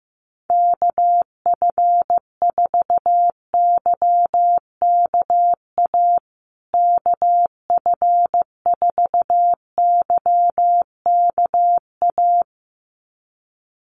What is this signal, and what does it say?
Morse code, heard as a keyed tone, about 15 wpm, 700 Hz. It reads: KF4YKA KF4YKA
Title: USA Call Signs Repeated 2x 15wpm